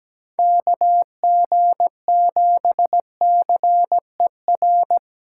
Morse code, heard as a keyed tone, about 17 wpm, 700 Hz. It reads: KG7CER
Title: USA Callsign DE Callsign 17wpm